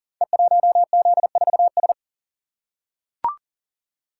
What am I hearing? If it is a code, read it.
Morse code, heard as a keyed tone, about 40 wpm, 700 Hz. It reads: E1Z4S
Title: Code Groups - 5 Letter:Numbers - 40wpm